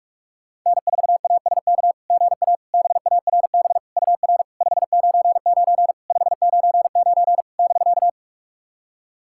Morse code, heard as a keyed tone, about 44 wpm, 690 Hz. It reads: N4ASK GA BARB UR 599 599 <BK>
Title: SOTA QSO Practice 45wpm